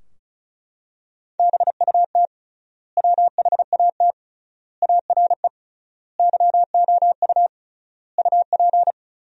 Morse code, heard as a keyed tone, about 36 wpm, 700 Hz. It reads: BUT WHAT ARE YOU UP
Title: 3x Word-Spacing - Sentences from Top 100 Words - 35wpm